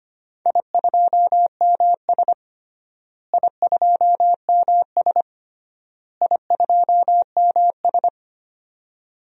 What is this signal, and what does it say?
Morse code, heard as a keyed tone, about 25 wpm, 700 Hz. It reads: I2MS I2MS I2MS
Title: DXCC-100 Call Signs - 4 Characters - Repeated 3x - 25wpm